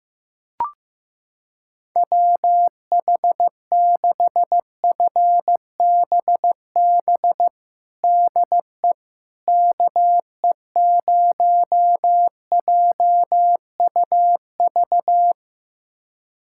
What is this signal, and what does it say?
Morse code, heard as a keyed tone, about 15 wpm, 700 Hz. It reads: WH6FBB DE KE0JUV
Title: USA Callsign DE Callsign 15wpm